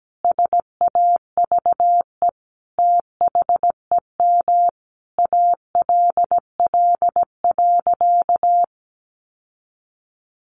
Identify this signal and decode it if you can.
Morse code, heard as a keyed tone, about 17 wpm, 700 Hz. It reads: SAVE THEM ALL.
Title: Sentences from Top 1100 Words 17wpm